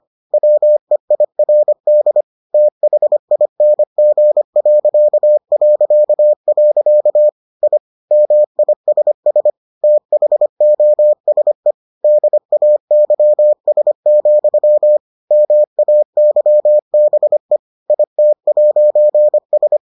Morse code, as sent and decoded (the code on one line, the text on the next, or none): .-- . .. .-. -.. / - .... .. -. --. .-.-.- .-.-.- .-.-.- / .. / -- .. ... ... / - .... --- ... / -.. .- -.-- ... --..-- / -- .- -.-- -... / .. - .----. ...
WEIRD THING... I MISS THOSE DAYS, MAYBE IT'S